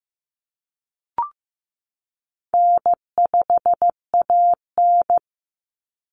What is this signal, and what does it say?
Morse code, heard as a keyed tone, about 15 wpm, 700 Hz. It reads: N5AN
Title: Call Signs Repeated 2x - 15wpm